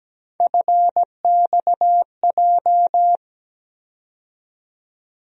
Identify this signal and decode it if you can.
Morse code, heard as a keyed tone, about 17 wpm, 700 Hz. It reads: FXJ
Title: USA Call Sign Postfixes - 3 Letters 17wpm